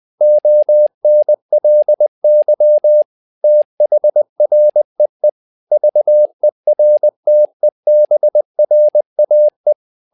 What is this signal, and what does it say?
Morse code, heard as a keyed tone, about 20 wpm, 600 Hz. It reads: ONLY THREE VERTEBRAE
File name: learning center / tower trouble / tower trouble 20wpm